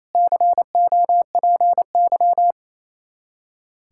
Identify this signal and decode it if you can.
Morse code, heard as a keyed tone, about 28 wpm, 700 Hz. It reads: COPY
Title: Practice QSO - Exchange 3 - More Info - 28wpm